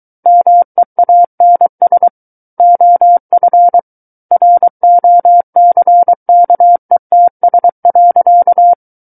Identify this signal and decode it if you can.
Morse code, heard as a keyed tone, about 23 wpm, 700 Hz. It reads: MEANS OF ROCKETS.
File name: Book One / 14-Chapter 14 - 23 WPM